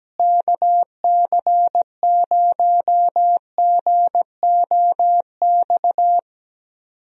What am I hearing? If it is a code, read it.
Morse code, heard as a keyed tone, about 17 wpm, 700 Hz. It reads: KC0GOX
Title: USA Call Signs - 6 Characters - Repeated 3x - 17wpm